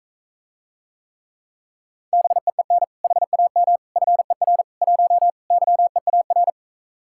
Morse code, heard as a keyed tone, about 45 wpm, 700 Hz. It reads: BEEN HAM FER 1 YEAR